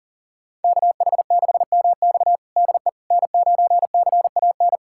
Morse code, heard as a keyed tone, about 40 wpm, 700 Hz. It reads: KH6MX DE N9CAN
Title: USA Callsign DE Callsign - 5 Characters - 40wpm